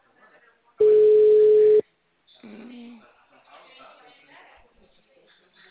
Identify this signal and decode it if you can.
Ringback tone (Europe, 425 Hz), with speech.